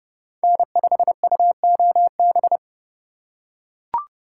Morse code, heard as a keyed tone, about 28 wpm, 700 Hz. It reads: N5UOB